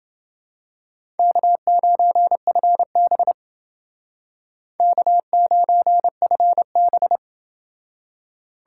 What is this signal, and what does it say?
Morse code, heard as a keyed tone, about 29 wpm, 700 Hz. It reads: K9FB K9FB